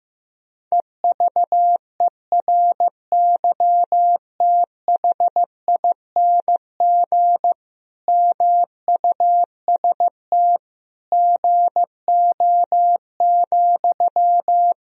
Morse code, heard as a keyed tone, about 15 wpm, 700 Hz. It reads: EVERYTHING MUST GO,
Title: Sentences from Top 1000 Words - Review Part 4 - 15wpm